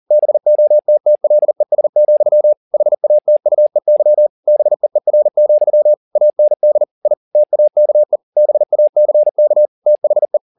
Morse code, heard as a keyed tone, about 40 wpm, 600 Hz. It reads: BOTTLES, HATUEY BEER, AND I TAKE BACK THE